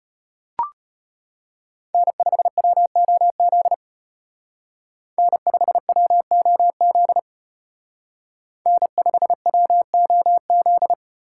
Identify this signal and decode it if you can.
Morse code, heard as a keyed tone, about 33 wpm, 700 Hz. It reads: N5WOZ N5WOZ N5WOZ